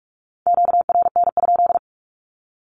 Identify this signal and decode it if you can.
Morse code, heard as a keyed tone, about 44 wpm, 700 Hz. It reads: QRN?
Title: Q-Codes Repeated 3x 45wpm